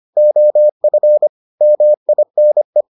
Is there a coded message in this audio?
Yes: Morse code from 0.2 to 2.8 s, heard as a keyed tone, about 25 wpm, 600 Hz. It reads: OF MINE